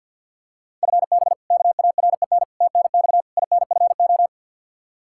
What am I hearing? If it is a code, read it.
Morse code, heard as a keyed tone, about 52 wpm, 700 Hz. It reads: FB KAREN TNX INFO
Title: K1USN SST QSO Practice 50wpm